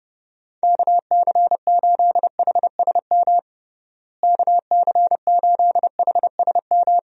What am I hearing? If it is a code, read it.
Morse code, heard as a keyed tone, about 30 wpm, 700 Hz. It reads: KC8HSM KC8HSM
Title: USA Call Signs - 6 Characters - Repeated 2x - 30wpm